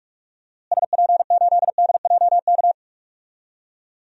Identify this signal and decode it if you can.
Morse code, heard as a keyed tone, about 46 wpm, 700 Hz. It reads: SP8DJK
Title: DXCC-100 Call Signs - 6 Characters - Repeated 3x - 45wpm